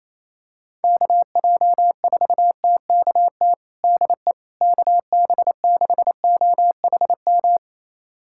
Morse code, heard as a keyed tone, about 28 wpm, 700 Hz. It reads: KJ4TKT DE KB6OHM
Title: USA Callsign DE Callsign 28wpm